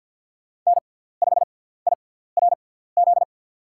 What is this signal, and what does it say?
Morse code, heard as a keyed tone, about 47 wpm, spaced as for 17 wpm, 700 Hz. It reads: N5IRZ
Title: USA Call Signs - 6x Character Spacing - 50wpm